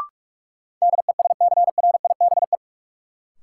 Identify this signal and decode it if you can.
Morse code, heard as a keyed tone, about 46 wpm, 700 Hz. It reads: DESCRIBE
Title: Sentences from Top 1000 Words - New Word Drill - 45wpm